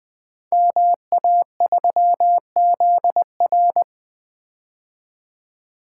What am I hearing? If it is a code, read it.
Morse code, heard as a keyed tone, about 20 wpm, 700 Hz. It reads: MA3ZR
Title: Code Groups - 5 Letter:Numbers - 20wpm